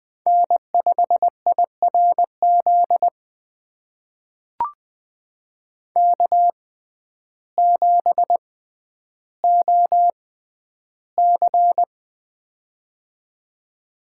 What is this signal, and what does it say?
Morse code, heard as a keyed tone, about 20 wpm, spaced as for 11 wpm, 700 Hz. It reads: N5IRZ K7OC
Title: USA Call Signs - 6x Character Spacing - 20wpm